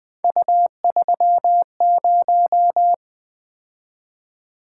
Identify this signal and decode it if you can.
Morse code, heard as a keyed tone, about 24 wpm, 700 Hz. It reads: U30
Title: Speed-Racing - Three Letter-Number Combinations - 20wpm